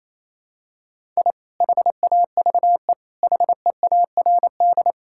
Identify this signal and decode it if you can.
Morse code, heard as a keyed tone, about 28 wpm, 700 Hz. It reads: I HAVE HEARD
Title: Sentences from Top 1000 Words - Review Part 6 - 28wpm